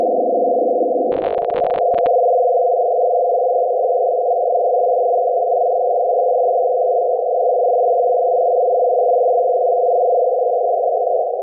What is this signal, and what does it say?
Two-tone teleprinter signal, about 200 Hz shift.